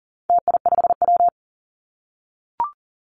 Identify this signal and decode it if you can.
Morse code, heard as a keyed tone, about 40 wpm, 700 Hz. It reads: TI5W
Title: Call Signs - 40wpm